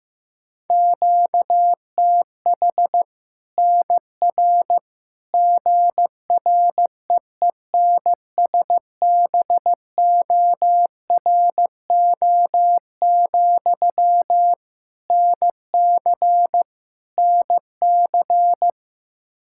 Morse code, heard as a keyed tone, about 16 wpm, 700 Hz. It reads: QTH NR GREENSBORO, NC NC
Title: Practice QSO - Exchange 1 - Beginning - 15wpm